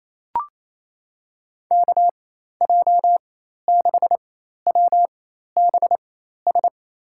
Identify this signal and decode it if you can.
Morse code, heard as a keyed tone, about 28 wpm, spaced as for 13 wpm, 700 Hz. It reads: KJ6WBS
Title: USA Call Signs - 6 Characters - 4x Character Spacing - 28wpm